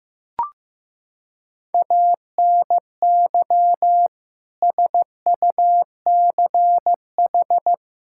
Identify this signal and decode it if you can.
Morse code, heard as a keyed tone, about 16 wpm, 700 Hz. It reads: ANY SUCH
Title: Sentences from Top 1600 Words 15wpm